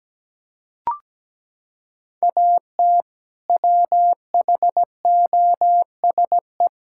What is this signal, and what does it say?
Morse code, heard as a keyed tone, about 17 wpm, 700 Hz. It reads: AT WHOSE
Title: Sentences from Top 1600 Words 17wpm